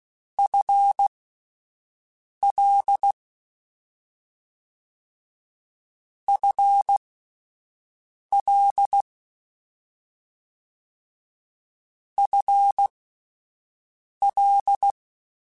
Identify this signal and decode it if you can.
Morse code, heard as a keyed tone, about 16 wpm, spaced as for 5 wpm, 800 Hz. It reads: FL FL FL
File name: Combinations / Combinations-mp3s / Combinations@5